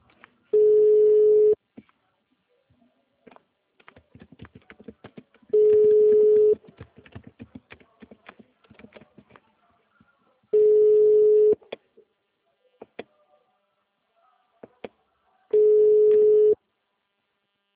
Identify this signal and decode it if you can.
Ringback tone (Europe, 425 Hz), with speech.